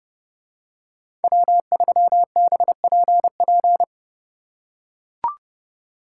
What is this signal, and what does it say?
Morse code, heard as a keyed tone, about 30 wpm, 700 Hz. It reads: W3BPP